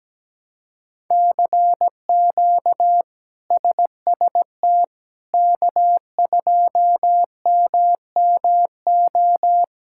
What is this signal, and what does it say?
Morse code, heard as a keyed tone, about 17 wpm, 700 Hz. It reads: CQ SST K2MMO
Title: K1USN SST QSO Practice - 17wpm